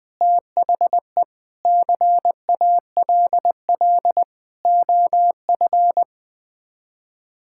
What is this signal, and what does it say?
Morse code, heard as a keyed tone, about 20 wpm, 700 Hz. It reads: THE CALL OF
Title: Sets of 8 Words - 20wpm